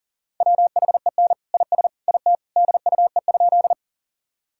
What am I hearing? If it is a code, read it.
Morse code, heard as a keyed tone, about 40 wpm, 700 Hz. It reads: WHEN IS IT DUE?
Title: Sentences from Top 1000 Words - Review Part 6 - 40wpm